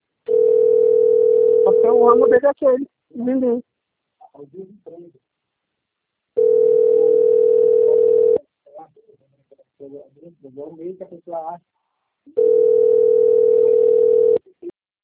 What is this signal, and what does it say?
Ringback tone (North America), 440 + 480 Hz, with speech.
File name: rg-1600-unknown- - - .45281